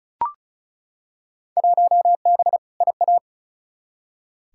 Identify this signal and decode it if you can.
Morse code, heard as a keyed tone, about 36 wpm, 700 Hz. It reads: 1B IA
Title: Field Day Exchanges Encore - 35wpm